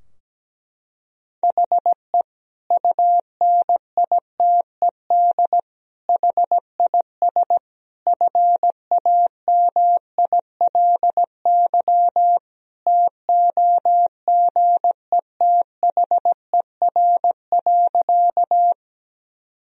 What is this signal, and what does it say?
Morse code, heard as a keyed tone, about 17 wpm, 700 Hz. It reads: HE UNITED HIS FAMILY TOGETHER.
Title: Sentences from Top 1000 Words - Review Part 4 - 17wpm